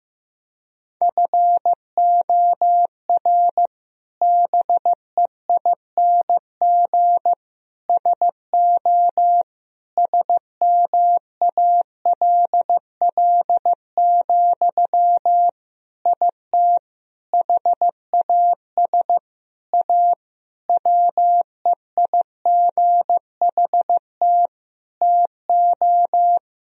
Morse code, heard as a keyed tone, about 15 wpm, 700 Hz. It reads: FOR BEING SO SMALL, IT HAS A WEIGHT TO
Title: Sentences from Top 1000 Words - Review Part 3 - 15wpm